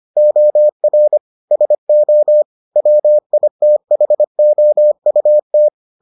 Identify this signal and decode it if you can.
Morse code, heard as a keyed tone, about 25 wpm, 600 Hz. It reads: OR SO WITHOUT